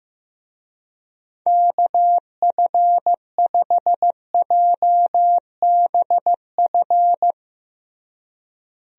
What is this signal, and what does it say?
Morse code, heard as a keyed tone, about 15 wpm, 700 Hz. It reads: KF5JBF
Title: USA Call Signs - 6 Characters - Repeated 2x - 15wpm